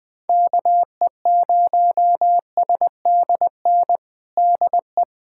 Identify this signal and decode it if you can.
Morse code, heard as a keyed tone, about 20 wpm, 700 Hz. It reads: KE0SDN DE
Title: USA Callsign DE Callsign - 6 Characters - 20wpm